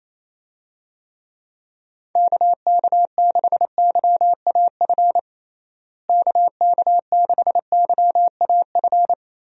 Morse code, heard as a keyed tone, about 28 wpm, 700 Hz. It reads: KK6YAF KK6YAF
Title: USA Call Signs - 6 Characters - Repeated 3x - 28wpm